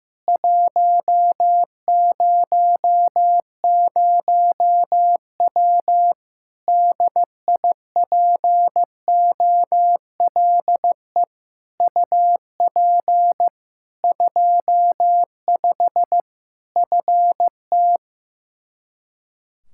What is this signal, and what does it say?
Morse code, heard as a keyed tone, about 16 wpm, 700 Hz. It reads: 100W DIPOLE UP 25 FT
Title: Practice QSO - Exchange 2 - Info - 15wpm